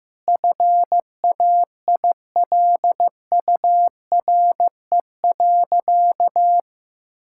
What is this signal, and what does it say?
Morse code, heard as a keyed tone, about 15 wpm, 700 Hz. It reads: FAILURE.
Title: Sentences from Top 1000 Words - Review Part 6 - 15wpm